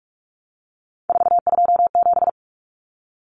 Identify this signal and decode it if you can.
Morse code, heard as a keyed tone, about 44 wpm, 700 Hz. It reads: 427
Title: Common Cut Numbers - 3-Digit 45wpm